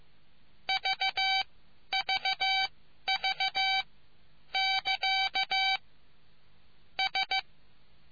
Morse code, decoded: VVV <KA> S

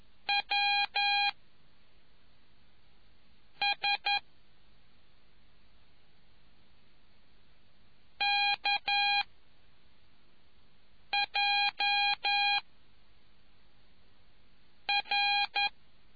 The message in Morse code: W S K J R

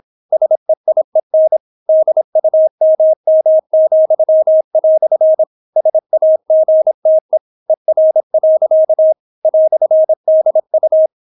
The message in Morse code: SEIEN DUMM," SAGTE ER. "DU

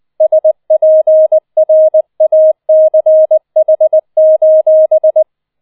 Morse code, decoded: SPRACH:

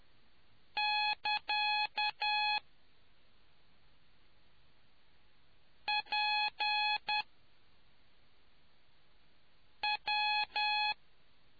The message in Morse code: <KA> PW

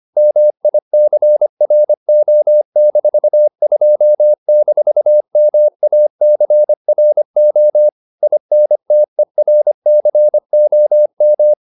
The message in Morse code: MICRO-2-MACRO INTERCOM

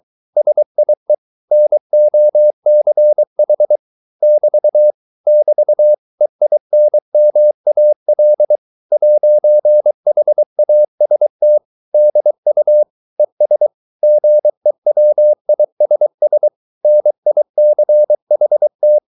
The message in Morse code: SIE NOCH <BT> <BT> EINMAL 'HAST DU ES GEWISS NICHT

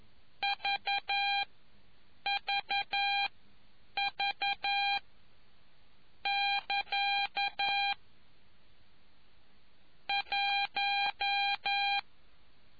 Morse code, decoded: VVV<KA> 1

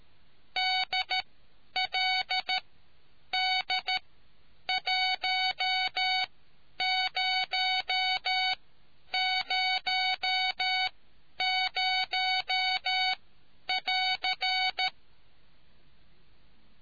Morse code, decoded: DLD 1000<AR>

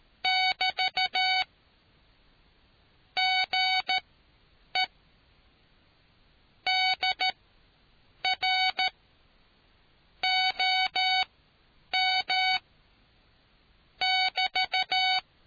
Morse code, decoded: <BT> GE DR OM <BT>

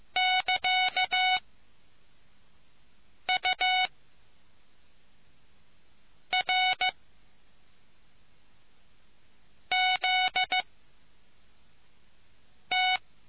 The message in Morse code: <KA> U R Z T

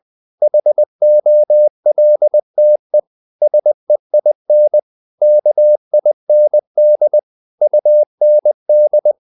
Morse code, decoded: HOLTE SEIN KIND UND

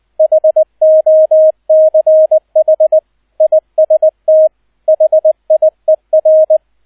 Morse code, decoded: HOCH IST HIER

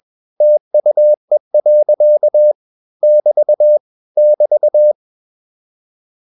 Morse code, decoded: TUE. <BT> <BT>

